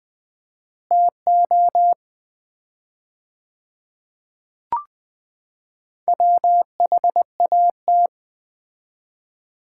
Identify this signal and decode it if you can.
Morse code, heard as a keyed tone, about 20 wpm, 700 Hz. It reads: TO WHAT
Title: 4x Word-Spacing - Sets of 5 Words - 20wpm